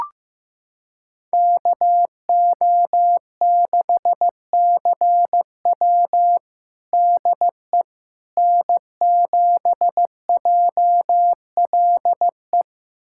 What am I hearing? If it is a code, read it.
Morse code, heard as a keyed tone, about 15 wpm, 700 Hz. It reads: KO6CW DE N7JLE
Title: USA Callsign DE Callsign - 5 Characters - 15wpm